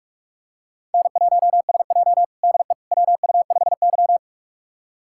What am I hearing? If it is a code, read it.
Morse code, heard as a keyed tone, about 47 wpm, 700 Hz. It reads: N1SJ DE WU5Y
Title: USA Callsign DE Callsign - 4 Characters - 45wpm